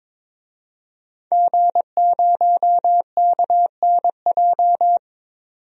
Morse code, heard as a keyed tone, about 22 wpm, 700 Hz. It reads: G0KNJ
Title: DXCC-100 Call Signs - 5 Characters - 22wpm